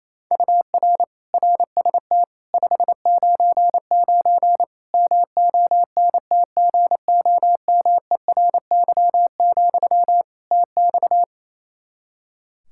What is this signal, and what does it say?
Morse code, heard as a keyed tone, about 28 wpm, 700 Hz. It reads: UR RST 599 MONTGOMERY, TX
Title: K1USN SST QSO Practice 28wpm